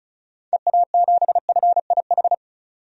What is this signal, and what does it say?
Morse code, heard as a keyed tone, about 34 wpm, 700 Hz. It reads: EA7FIH
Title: DXCC-100 Call Signs - 6 Characters - 35wpm